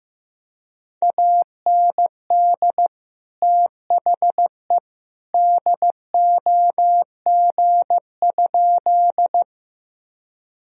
Morse code, heard as a keyed tone, about 15 wpm, 700 Hz. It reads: AND THE DOG?
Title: Sentences from Top 1000 Words - Review Part 2 - 15wpm